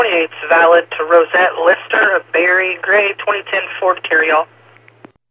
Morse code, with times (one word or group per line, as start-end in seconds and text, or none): none